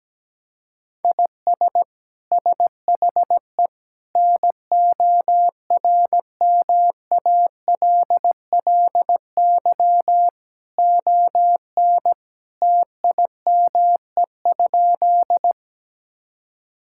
Morse code, heard as a keyed tone, about 17 wpm, 700 Hz. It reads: IS SHE NORMALLY ON TIME?
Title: Sentences from Top 1000 Words - Review Part 6 - 17wpm